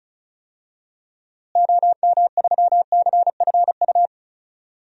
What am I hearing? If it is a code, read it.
Morse code, heard as a keyed tone, about 34 wpm, 700 Hz. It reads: OM3CFU
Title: DXCC-100 Call Signs - 6 Characters - Repeated 3x - 35wpm